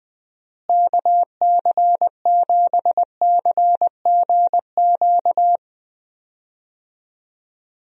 Morse code, heard as a keyed tone, about 20 wpm, 700 Hz. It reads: KC7CGQ
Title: USA Call Signs - 6 Characters - Repeated 2x - 20wpm